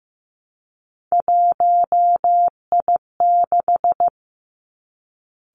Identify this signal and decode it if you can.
Morse code, heard as a keyed tone, about 15 wpm, 700 Hz. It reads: 1I6
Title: Three Challenging Letter-Number Combinations 15wpm